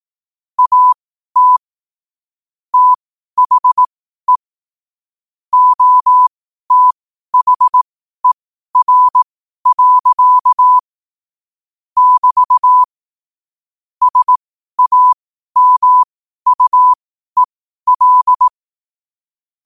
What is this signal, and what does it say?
Morse code, heard as a keyed tone, about 18 wpm, spaced as for 12 wpm, 1000 Hz. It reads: AT THE OTHER. <BT> SAMUEL
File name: qotdfiles / 12wpm